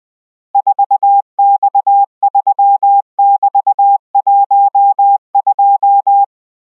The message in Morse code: 4X3<BT>12